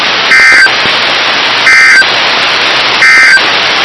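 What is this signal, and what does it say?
Emergency Alert System (SAME) tones, heard as harsh buzzing bursts. It is heard from 0.2 to 3.5 s.
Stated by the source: NOAA Weather Radio station KIH54.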